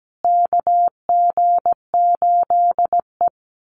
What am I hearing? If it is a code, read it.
Morse code, heard as a keyed tone, about 17 wpm, 700 Hz. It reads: KG8E
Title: USA Call Signs Repeated 3x 17wpm